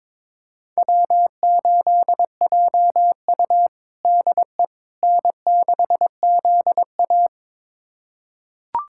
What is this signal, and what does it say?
Morse code, heard as a keyed tone, about 21 wpm, 700 Hz. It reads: W8JU DE N6ZA
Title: USA Callsign DE Callsign - 4 Characters - 22wpm